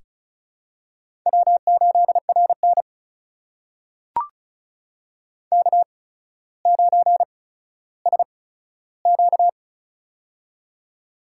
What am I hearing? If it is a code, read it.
Morse code, heard as a keyed tone, about 34 wpm, spaced as for 15 wpm, 700 Hz. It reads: W8RN K9SQ
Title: USA Call Signs - 4 Characters - 8x Character Spacing - 35wpm